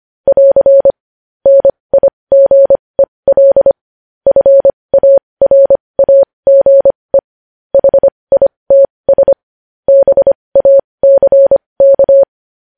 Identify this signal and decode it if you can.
Morse code, heard as a keyed tone, about 25 wpm, spaced as for 20 wpm, 550 Hz. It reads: <AR> NIGEL FARAGE HITS BACK
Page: News Headlines in Morse Code at 20 WPM